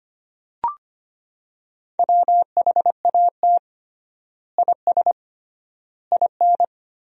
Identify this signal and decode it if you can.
Morse code, heard as a keyed tone, about 25 wpm, 700 Hz. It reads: WHAT IS IN
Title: 3x Word-Spacing - Sentences from Top 200 Words - 25wpm